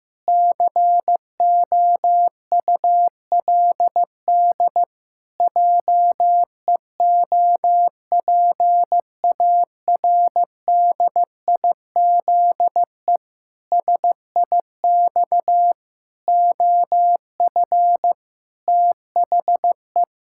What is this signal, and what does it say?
Morse code, heard as a keyed tone, about 15 wpm, 700 Hz. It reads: COULD JEOPARDIZE SIX OF THE
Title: Pangram Challenge v2 - 15wpm